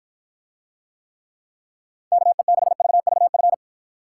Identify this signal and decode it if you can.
Morse code, heard as a keyed tone, about 53 wpm, 700 Hz. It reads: KE6VVF